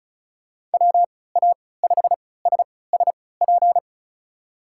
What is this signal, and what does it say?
Morse code, heard as a keyed tone, about 36 wpm, spaced as for 20 wpm, 700 Hz. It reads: WA5SSP
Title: USA Call Signs - 3x Character Spacing - 35wpm